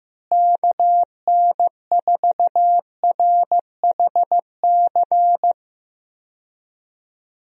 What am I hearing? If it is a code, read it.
Morse code, heard as a keyed tone, about 15 wpm, 700 Hz. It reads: KN4RHC
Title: USA Call Signs - 6 Characters - Repeated 3x - 15wpm